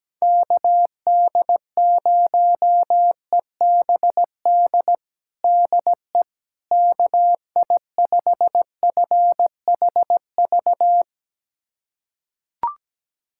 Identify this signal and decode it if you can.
Morse code, heard as a keyed tone, about 17 wpm, 700 Hz. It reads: KD0EBD DE KI5FHV
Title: USA Callsign DE Callsign - 6 Characters - 17wpm